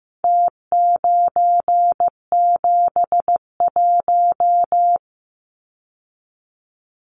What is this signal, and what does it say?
Morse code, heard as a keyed tone, about 15 wpm, 700 Hz. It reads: T971
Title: Numbers - Contest Style - 4-Digit 15wpm